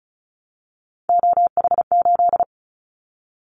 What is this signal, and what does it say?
Morse code, heard as a keyed tone, about 34 wpm, 700 Hz. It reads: OH8
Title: Callsign Prefixes for Top 25 DXCC - 35wpm